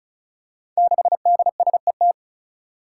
Morse code, heard as a keyed tone, about 34 wpm, 700 Hz. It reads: 6DSET
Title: Five Challenging Letter-Number Combinations 35wpm